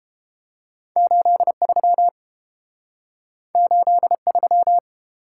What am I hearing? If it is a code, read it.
Morse code, heard as a keyed tone, about 31 wpm, 700 Hz. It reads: 83 83